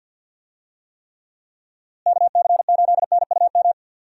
Morse code, heard as a keyed tone, about 51 wpm, 700 Hz. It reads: KC8NUM
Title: USA Call Signs - 6 Characters - Repeated 3x - 50wpm